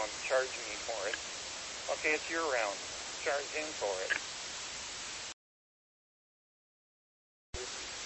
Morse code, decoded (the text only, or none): none